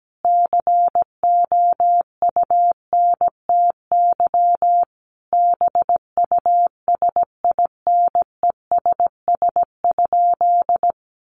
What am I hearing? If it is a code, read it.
Morse code, heard as a keyed tone, about 17 wpm, 700 Hz. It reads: COUNTY BUSINESS?
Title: Sentences from Top 1200 Words 17wpm